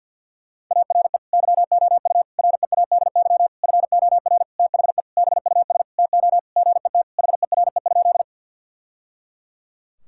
Morse code, heard as a keyed tone, about 49 wpm, 700 Hz. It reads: ARE YOU READY FOR THE BUS TO GET HERE?
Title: Sentences from Top 1000 Words - Review Part 6 - 50wpm